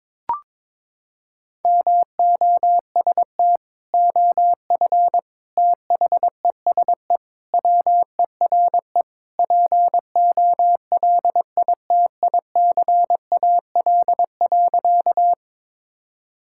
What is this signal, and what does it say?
Morse code, heard as a keyed tone, about 22 wpm, 700 Hz. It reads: MOST OF THESE WERE POLITICAL.
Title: Sentences from Top 1000 Words - Review Part 5 - 22wpm